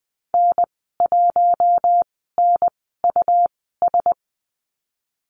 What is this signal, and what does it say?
Morse code, heard as a keyed tone, about 20 wpm, spaced as for 15 wpm, 700 Hz. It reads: N1NUS
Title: 193 Lesson 31g - Letter v - Call Signs 20wpm